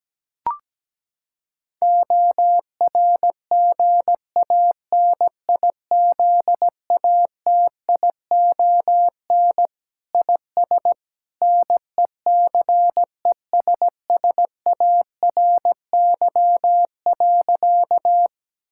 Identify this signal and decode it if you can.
Morse code, heard as a keyed tone, about 17 wpm, 700 Hz. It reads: ORGANIZATION IS NECESSARY.
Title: Sentences from Top 1000 Words - Review Part 5 - 17wpm